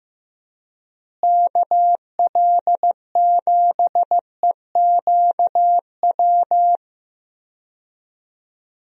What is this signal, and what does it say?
Morse code, heard as a keyed tone, about 16 wpm, 700 Hz. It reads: KL7EQW
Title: DXCC-100 Call Signs - 6 Characters - Repeated 3x - 15wpm